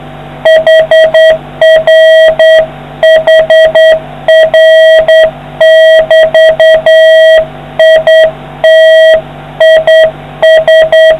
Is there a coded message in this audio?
Yes: Morse code from 0.5 to 11.1 s, heard as a keyed tone, about 8 wpm, 640 Hz. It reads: HRHR<BT> IT IS